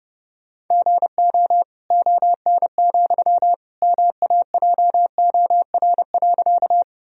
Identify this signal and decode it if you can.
Morse code, heard as a keyed tone, about 30 wpm, 700 Hz. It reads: GO ON, MAJOR.
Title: Sentences from Top 1000 Words - Review Part 4 - 30wpm